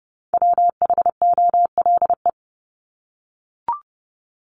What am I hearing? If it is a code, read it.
Morse code, heard as a keyed tone, about 30 wpm, 700 Hz. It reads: WHOLE
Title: Top 500 Words 30wpm